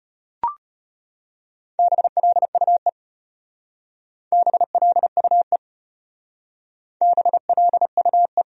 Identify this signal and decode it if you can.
Morse code, heard as a keyed tone, about 33 wpm, 700 Hz. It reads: BLUE BLUE BLUE